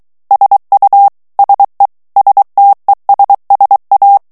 Morse code, heard as a keyed tone, about 23 wpm, 800 Hz. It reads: SU SE STESSA
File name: morse mp3 / ITALIEN texte clair 1 vitesse 22 mots minute